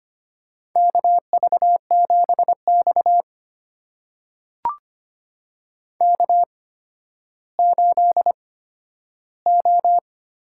Morse code, heard as a keyed tone, about 25 wpm, spaced as for 12 wpm, 700 Hz. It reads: KV7X K8O
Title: USA Call Signs - 8x Character Spacing - 25wpm